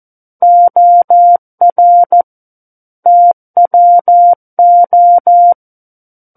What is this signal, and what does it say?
Morse code, heard as a keyed tone, about 14 wpm, 700 Hz. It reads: OR TWO
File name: Book One / 05-Chapter 5 - 14 WPM